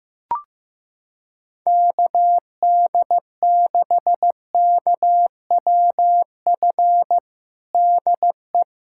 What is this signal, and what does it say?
Morse code, heard as a keyed tone, about 16 wpm, 700 Hz. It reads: KD6KWF DE
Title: USA Callsign DE Callsign - 6 Characters - 15wpm